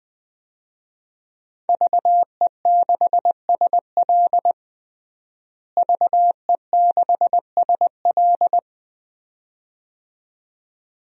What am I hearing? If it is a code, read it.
Morse code, heard as a keyed tone, about 20 wpm, 700 Hz. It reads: VE6SL VE6SL
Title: DXCC-100 Call Signs - 5 Characters - Repeated 2x - 20wpm